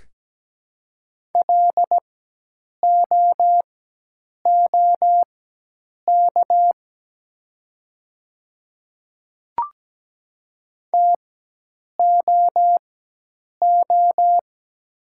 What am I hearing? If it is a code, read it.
Morse code, heard as a keyed tone, about 17 wpm, spaced as for 8 wpm, 700 Hz. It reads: LOOK TOO
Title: 4x Character Spacing - Top 500 Words - 17wpm